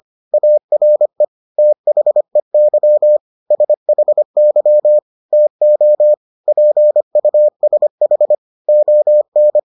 Morse code, heard as a keyed tone, about 25 wpm, 600 Hz. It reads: ARE THEY SHY TO PUSH ON